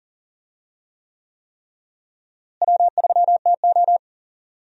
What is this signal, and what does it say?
Morse code, heard as a keyed tone, about 39 wpm, 700 Hz. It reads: W3TO
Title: USA Call Signs - 4 Characters - Repeated 3x - 40wpm